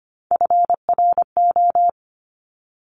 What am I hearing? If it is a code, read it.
Morse code, heard as a keyed tone, about 25 wpm, 700 Hz. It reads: FRO
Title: Top Three Letter Combinations in Words 25wpm